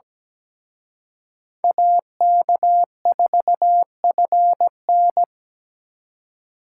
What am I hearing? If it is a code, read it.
Morse code, heard as a keyed tone, about 20 wpm, 700 Hz. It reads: AK4FN
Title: Speed Racing - USA 2x2 Call Signs 17wpm